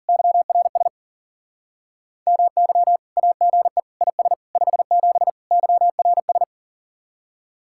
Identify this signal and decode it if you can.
Morse code, heard as a keyed tone, about 52 wpm, 700 Hz. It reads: YRS MY AGE IS 57 YRS